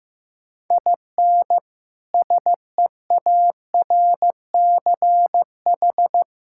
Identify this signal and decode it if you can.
Morse code, heard as a keyed tone, about 15 wpm, 700 Hz. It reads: IN SEARCH
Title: Sentences from Top 1000 Words - Review Part 5 - 15wpm